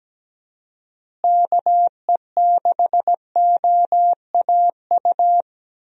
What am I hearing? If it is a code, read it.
Morse code, heard as a keyed tone, about 17 wpm, 700 Hz. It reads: KE6OAU
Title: USA Call Signs - 6 Characters - Repeated 3x - 17wpm